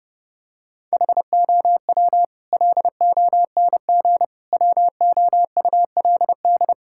Morse code, heard as a keyed tone, about 32 wpm, 700 Hz. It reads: HOW LONG WOULD
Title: Sentences from Top 1000 Words - Review Part 5 - 30wpm